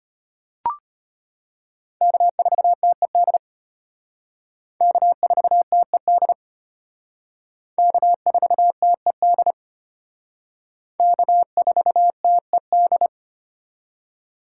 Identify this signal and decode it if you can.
Morse code, heard as a keyed tone, about 32 wpm, 700 Hz. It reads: K4TED K4TED K4TED K4TED